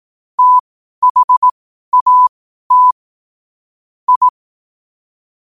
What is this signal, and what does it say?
Morse code, heard as a keyed tone, about 18 wpm, spaced as for 12 wpm, 1000 Hz. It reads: THAT I
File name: qotdfiles / 12wpm